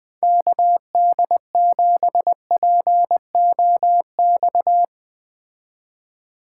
Morse code, heard as a keyed tone, about 20 wpm, 700 Hz. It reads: KD7POX
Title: Speed Racing - USA 2x3 Call Signs 17wpm